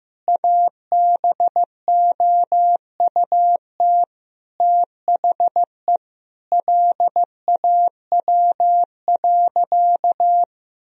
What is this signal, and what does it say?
Morse code, heard as a keyed tone, about 15 wpm, 700 Hz. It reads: ABOUT THE LAW.
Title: Sentences from Top 1000 Words - Review Part 3 - 15wpm